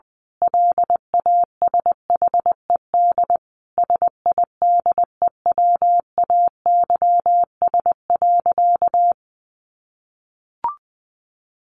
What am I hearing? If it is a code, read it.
Morse code, heard as a keyed tone, about 20 wpm, 700 Hz. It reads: LASHED SIDEWAYS.
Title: A Sound Of Thunder 20wpm